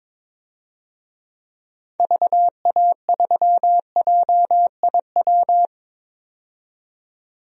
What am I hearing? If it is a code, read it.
Morse code, heard as a keyed tone, about 22 wpm, 700 Hz. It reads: VA3JIW
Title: DXCC-100 Call Signs - 6 Characters - 22wpm